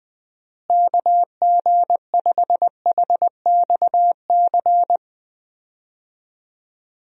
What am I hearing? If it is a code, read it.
Morse code, heard as a keyed tone, about 20 wpm, 700 Hz. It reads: KG5HXC
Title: USA Call Signs - 6 Characters - Repeated 3x - 20wpm